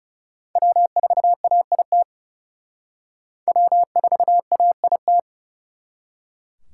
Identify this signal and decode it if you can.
Morse code, heard as a keyed tone, about 37 wpm, 700 Hz. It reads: W4AIT W4AIT
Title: Speed Racing - USA 1x3 Call Signs 30wpm